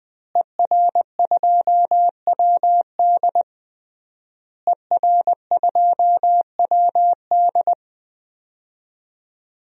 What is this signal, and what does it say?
Morse code, heard as a keyed tone, about 20 wpm, 700 Hz. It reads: ER2WD ER2WD
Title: DXCC-100 Call Signs - 5 Characters - Repeated 3x - 20wpm